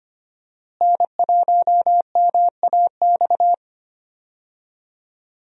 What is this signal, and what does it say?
Morse code, heard as a keyed tone, about 25 wpm, 700 Hz. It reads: N1MAX
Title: USA Call Signs - 5 Characters - Repeated 3x - 25wpm